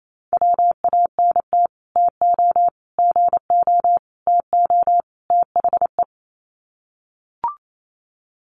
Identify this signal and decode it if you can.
Morse code, heard as a keyed tone, about 28 wpm, 700 Hz. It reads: WANT TO GO TO THE
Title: Sets of 5 Words Encore2 28wpm